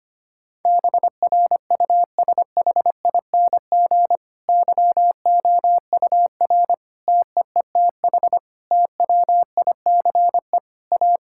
Morse code, heard as a keyed tone, about 25 wpm, 700 Hz. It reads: BRUSHING YOUR TEETH TWICE A
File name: practice / 25wpm / Difficult Sentences with Hints - 25wpm